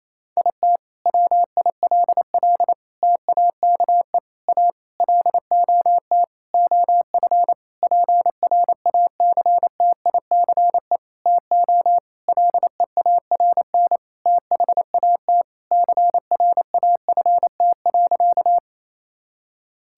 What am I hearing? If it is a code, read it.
Morse code, heard as a keyed tone, about 28 wpm, 700 Hz. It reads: IT WILL TAKE A LOT OF PRACTICE TO LEARN THAT CRAFT.
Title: Sentences from Top 1000 Words - Review Part 6 - 28wpm